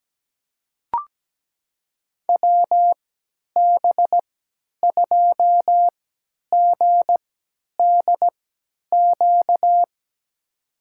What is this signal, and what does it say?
Morse code, heard as a keyed tone, about 17 wpm, spaced as for 10 wpm, 700 Hz. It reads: WB2GDQ
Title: USA Call Signs - 6 Characters - 3x Character Spacing - 17wpm